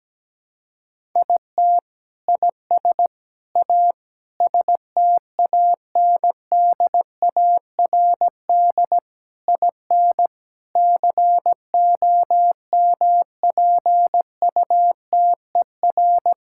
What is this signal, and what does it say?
Morse code, heard as a keyed tone, about 17 wpm, 700 Hz. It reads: IT IS A STANDARD IN COMPUTER
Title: Sentences from Top 1000 Words - Review Part 6 - 17wpm